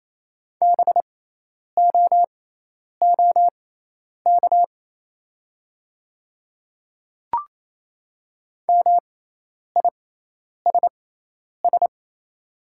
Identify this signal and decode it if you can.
Morse code, heard as a keyed tone, about 28 wpm, spaced as for 10 wpm, 700 Hz. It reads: BOOK MISS
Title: 6x Character Spacing - Top 500 Words - 28wpm